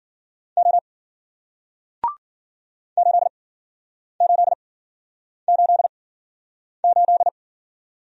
Morse code, heard as a keyed tone, about 54 wpm, spaced as for 9 wpm, 700 Hz. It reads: O 8888